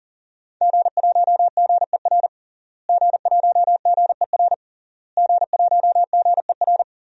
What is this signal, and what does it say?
Morse code, heard as a keyed tone, about 40 wpm, 700 Hz. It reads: G1GER G1GER G1GER
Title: DXCC-100 Call Signs - 5 Characters - Repeated 3x - 40wpm